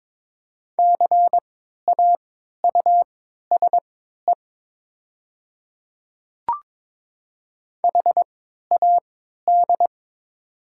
Morse code, heard as a keyed tone, about 22 wpm, spaced as for 12 wpm, 700 Hz. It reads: CAUSE HAD
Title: 3x Character Spacing - Top 200 Words - 22wpm